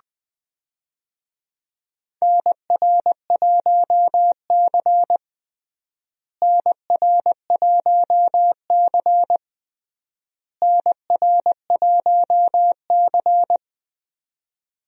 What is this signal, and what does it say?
Morse code, heard as a keyed tone, about 20 wpm, 700 Hz. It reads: NR1C NR1C NR1C
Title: USA Call Signs - 4 Characters - Repeated 3x - 20wpm